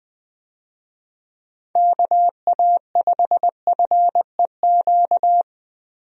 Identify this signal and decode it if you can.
Morse code, heard as a keyed tone, about 20 wpm, 700 Hz. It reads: KA5FEQ
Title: USA Call Signs - 6 Characters - Repeated 3x - 20wpm